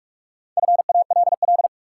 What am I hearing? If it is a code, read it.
Morse code, heard as a keyed tone, about 46 wpm, 700 Hz. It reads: FALL
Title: Sentences from Top 500 Words - New Word Drill - 45wpm